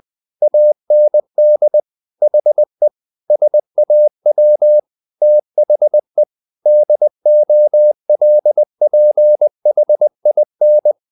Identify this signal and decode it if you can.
Morse code, heard as a keyed tone, about 20 wpm, 600 Hz. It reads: AND HE SAW THE DOLPHIN
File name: the old man and the sea part 2 20wpm